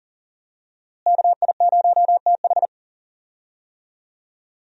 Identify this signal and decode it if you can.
Morse code, heard as a keyed tone, about 40 wpm, 700 Hz. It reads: KI0TH
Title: USA Call Signs - 5 Characters - Repeated 3x - 40wpm